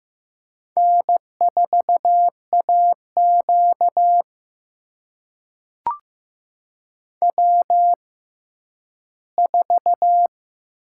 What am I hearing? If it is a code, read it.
Morse code, heard as a keyed tone, about 15 wpm, spaced as for 10 wpm, 700 Hz. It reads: N4AQ W4
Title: USA Call Signs - 4 Characters - 6x Character Spacing - 15wpm